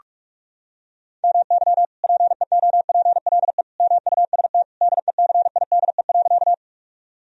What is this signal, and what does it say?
Morse code, heard as a keyed tone, about 46 wpm, 700 Hz. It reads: MY PEOPLE MUST DECIDE.
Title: Sentences from Top 1000 Words - Review Part 3 - 45wpm